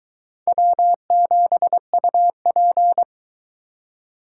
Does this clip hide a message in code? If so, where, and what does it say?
Yes: Morse code from 0.5 to 3.0 s, heard as a keyed tone, about 20 wpm, 700 Hz. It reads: W7UP